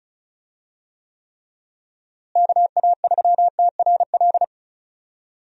Morse code, heard as a keyed tone, about 36 wpm, 700 Hz. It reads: KA3TRL
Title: USA Call Signs - 6 Characters - Repeated 3x - 35wpm